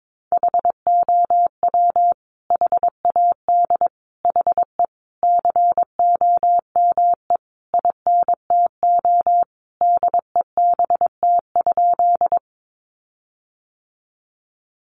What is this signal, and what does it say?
Morse code, heard as a keyed tone, about 22 wpm, 700 Hz. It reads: HOW HAD HE COME INTO DEBT?
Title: Sentences from Top 1300 Words 22wpm